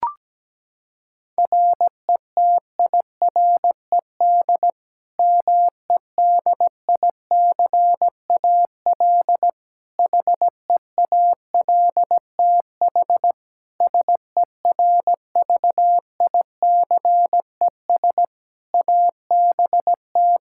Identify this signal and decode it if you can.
Morse code, heard as a keyed tone, about 17 wpm, 700 Hz. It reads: RETIRED MEDICAL HEALTH SERVICES ABT
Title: K1USN SST QSO Practice 17wpm